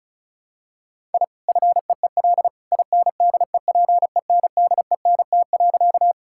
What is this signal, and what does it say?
Morse code, heard as a keyed tone, about 34 wpm, 700 Hz. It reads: I FEEL INDEPENDENT.
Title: Sentences from Top 1000 Words - Review Part 6 - 35wpm